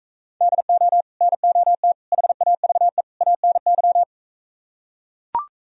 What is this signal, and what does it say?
Morse code, heard as a keyed tone, about 45 wpm, 700 Hz. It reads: DO NOT HAVE ANY